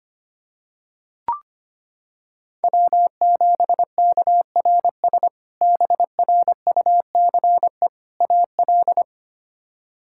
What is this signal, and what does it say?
Morse code, heard as a keyed tone, about 25 wpm, 700 Hz. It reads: W7KRS BRUCE AL
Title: K1USN SST QSO Practice - 25wpm